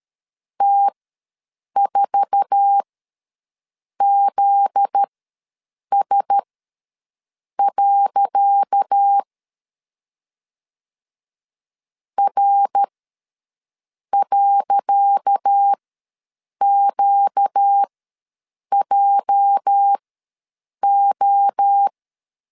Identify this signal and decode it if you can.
Morse code, heard as a keyed tone, about 13 wpm, spaced as for 6 wpm, 800 Hz. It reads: T4ZS. R.QJO
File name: LEZIONI / 10 SIGNS / 10C DOT ws 12wpm 3548